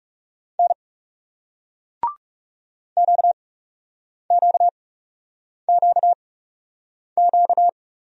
Morse code, heard as a keyed tone, about 39 wpm, spaced as for 8 wpm, 700 Hz. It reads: N QQQQ